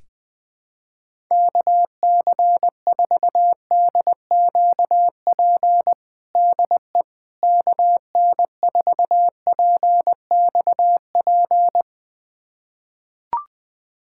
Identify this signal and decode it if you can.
Morse code, heard as a keyed tone, about 21 wpm, 700 Hz. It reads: KC4DQP DE KN4PXP
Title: USA Callsign DE Callsign 20wpm